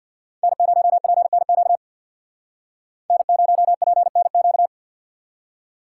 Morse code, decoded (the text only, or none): N0PNQ N0PNQ